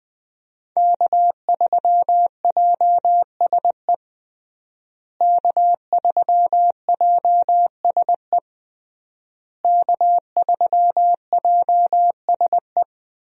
Morse code, heard as a keyed tone, about 20 wpm, 700 Hz. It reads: K3JSE K3JSE K3JSE
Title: USA Call Signs - 5 Characters - Repeated 3x - 20wpm